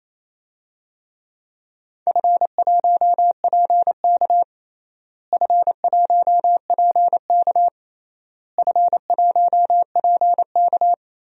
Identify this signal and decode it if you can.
Morse code, heard as a keyed tone, about 27 wpm, 700 Hz. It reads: F1PK F1PK F1PK
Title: DXCC-100 Call Signs - 4 Characters - Repeated 3x - 28wpm